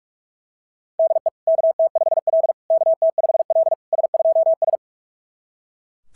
Morse code, heard as a keyed tone, about 41 wpm, 660 Hz. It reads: DE KT5L KT5L S2S
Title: SOTA QSO Practice 45wpm